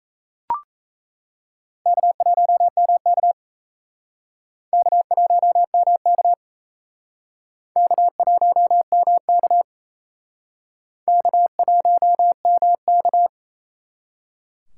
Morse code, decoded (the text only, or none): K1MK K1MK K1MK K1MK